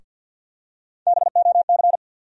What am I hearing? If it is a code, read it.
Morse code, heard as a keyed tone, about 53 wpm, 700 Hz. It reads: BOC
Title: USA Call Sign Postfixes - 3 Letters 50wpm